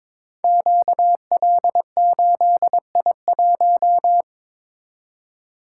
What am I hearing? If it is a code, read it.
Morse code, heard as a keyed tone, about 22 wpm, 690 Hz. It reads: QL8I1
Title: Code Groups - 5 Letter:Numbers - 22wpm